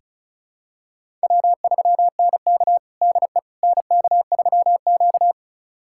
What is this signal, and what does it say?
Morse code, heard as a keyed tone, about 36 wpm, 700 Hz. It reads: W3NK DE NK3Q
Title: USA Callsign DE Callsign - 4 Characters - 35wpm